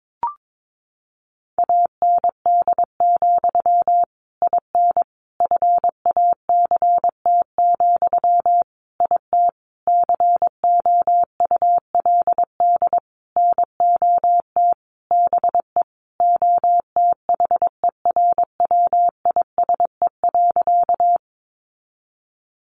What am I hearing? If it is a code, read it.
Morse code, heard as a keyed tone, about 22 wpm, 700 Hz. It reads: AND, IN FACT, IT COULD NOT BE OTHERWISE.
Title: Sentences from Top 1200 Words 22wpm